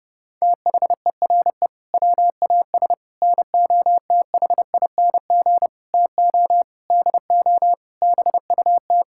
Morse code, heard as a keyed tone, about 30 wpm, 700 Hz. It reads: THERE WAS NOTHING TO DO BUT
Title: Sentences from Top 1000 Words - Review Part 6 - 30wpm